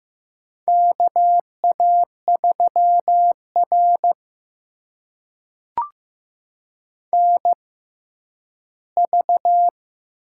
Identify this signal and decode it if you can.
Morse code, heard as a keyed tone, about 15 wpm, spaced as for 10 wpm, 700 Hz. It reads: KA3R NV
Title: USA Call Signs - 4 Characters - 6x Character Spacing - 15wpm